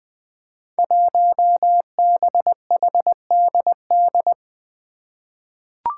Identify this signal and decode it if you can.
Morse code, heard as a keyed tone, about 20 wpm, 700 Hz. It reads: 1BHDD
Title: Five Challenging Letter-Number Combinations 20wpm